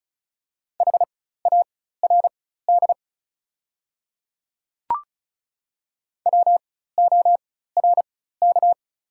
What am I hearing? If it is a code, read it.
Morse code, heard as a keyed tone, about 36 wpm, spaced as for 16 wpm, 700 Hz. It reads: HARD WORK
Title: 4x Character Spacing - Top 300 Words - 35wpm